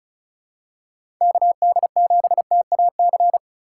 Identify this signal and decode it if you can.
Morse code, heard as a keyed tone, about 36 wpm, 700 Hz. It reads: KD7TAC
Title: USA Call Signs - 6 Characters - 3x Character Spacing - 35wpm